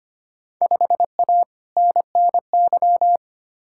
Morse code, heard as a keyed tone, about 25 wpm, 700 Hz. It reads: 5A NNY
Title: Field Day Exchanges - 25wpm